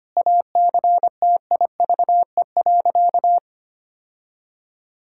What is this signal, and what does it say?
Morse code, heard as a keyed tone, about 25 wpm, 700 Hz. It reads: ACTIVE.
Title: Sentences from Top 1000 Words - Review Part 6 - 25wpm